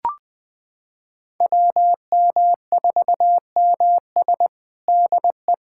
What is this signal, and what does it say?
Morse code, heard as a keyed tone, about 21 wpm, 700 Hz. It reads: WM4MS DE
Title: USA Callsign DE Callsign 20wpm